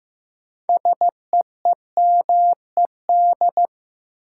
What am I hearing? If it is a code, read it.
Morse code, heard as a keyed tone, about 15 wpm, 700 Hz. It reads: SEEMED
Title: Sets of 3 Words - New Word Drill - 15wpm